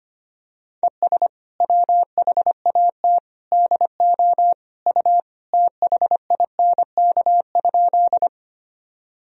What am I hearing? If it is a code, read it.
Morse code, heard as a keyed tone, about 25 wpm, 700 Hz. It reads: ES WHAT DO U THINK?
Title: Sentences from Top 200 Words and Abbreviations 25wpm